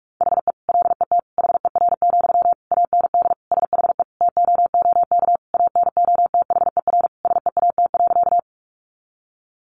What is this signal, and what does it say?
Morse code, heard as a keyed tone, about 44 wpm, 700 Hz. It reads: HE LET HER, AND SHE TOOK ANOTHER SEAT.
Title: Sentences from Top 1500 Words 45wpm